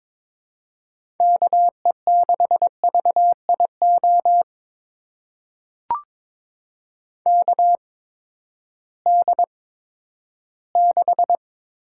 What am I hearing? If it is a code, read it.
Morse code, heard as a keyed tone, about 22 wpm, spaced as for 13 wpm, 700 Hz. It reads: KE6VIO KD6
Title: USA Call Signs - 6 Characters - 8x Character Spacing - 22wpm